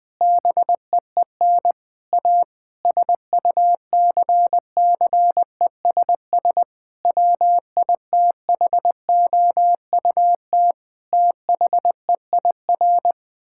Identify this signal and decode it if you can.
Morse code, heard as a keyed tone, about 20 wpm, 700 Hz. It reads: BEEN A SUCCESS WITHOUT THEIR
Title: Sentences from Top 1000 Words - Review Part 6 - 20wpm